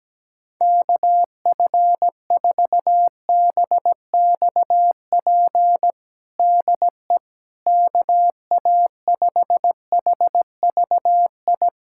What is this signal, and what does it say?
Morse code, heard as a keyed tone, about 17 wpm, 700 Hz. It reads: KF4BXP DE KA5HVI
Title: USA Callsign DE Callsign - 6 Characters - 17wpm